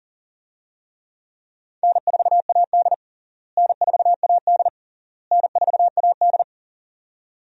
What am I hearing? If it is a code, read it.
Morse code, heard as a keyed tone, about 40 wpm, 700 Hz. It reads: N4AD N4AD N4AD
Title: USA Call Signs - 4 Characters - Repeated 3x - 40wpm